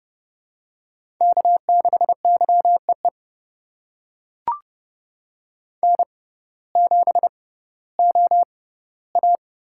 Morse code, heard as a keyed tone, about 30 wpm, spaced as for 16 wpm, 700 Hz. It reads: K6YEE N7OA
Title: USA Call Signs - 5 Characters - 6x Character Spacing - 30wpm